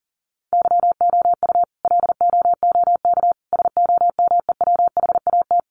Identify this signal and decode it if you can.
Morse code, heard as a keyed tone, about 40 wpm, 700 Hz. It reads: YOU LOOK SOMEWHAT
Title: Sentences from Top 1300 Words 40wpm